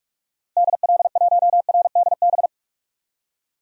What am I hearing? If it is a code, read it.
Morse code, heard as a keyed tone, about 47 wpm, 700 Hz. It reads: DL1RDB
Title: DXCC-100 Call Signs - 6 Characters - Repeated 3x - 45wpm